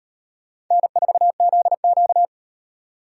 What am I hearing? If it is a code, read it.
Morse code, heard as a keyed tone, about 35 wpm, 700 Hz. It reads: N4ZQ